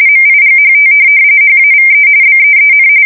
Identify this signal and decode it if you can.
RTTY, about 45.45 baud. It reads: MIGHT HAVE BEEN
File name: fsk170 / USS-New-Jersey